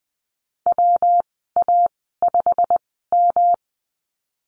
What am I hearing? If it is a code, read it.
Morse code, heard as a keyed tone, about 20 wpm, spaced as for 15 wpm, 700 Hz. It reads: WA5M
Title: 173 Lesson 28g - Number 9 - Call Signs 20wpm